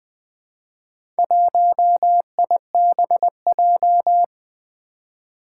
Morse code, heard as a keyed tone, about 20 wpm, 700 Hz. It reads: 1IBJ
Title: Four Challenging Letter-Number Combinations 20wpm